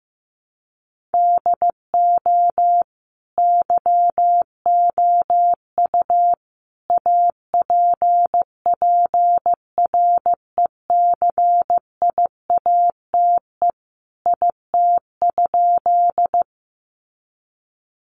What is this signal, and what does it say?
Morse code, heard as a keyed tone, about 15 wpm, 700 Hz. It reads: DO YOU APPRECIATE IT?
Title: Sentences from Top 1300 Words 15wpm